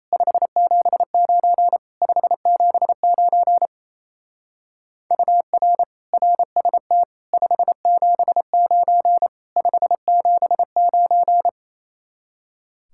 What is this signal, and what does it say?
Morse code, heard as a keyed tone, about 34 wpm, 700 Hz. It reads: 579 579 UR RST 579 579
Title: Speed-Racing Common QSO Phrases Encore - 28wpm